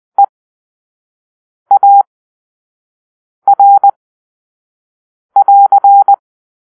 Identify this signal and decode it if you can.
Morse code, heard as a keyed tone, about 20 wpm, spaced as for 5 wpm, 800 Hz. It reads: EAR<AR>
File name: arrl / Grp-1 Drill-1